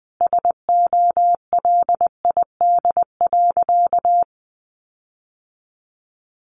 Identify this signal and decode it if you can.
Morse code, heard as a keyed tone, about 20 wpm, 700 Hz. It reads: SOLID.
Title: Sentences from Top 1500 Words 20wpm